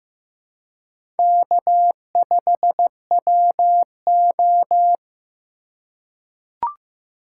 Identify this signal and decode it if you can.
Morse code, heard as a keyed tone, about 15 wpm, 700 Hz. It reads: K5WO
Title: USA Call Signs - 4 Characters - 3x Character Spacing - 15wpm